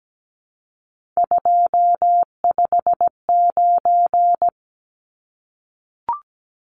Morse code, heard as a keyed tone, about 17 wpm, 700 Hz. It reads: 259
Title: Numbers - 3-Digit 17wpm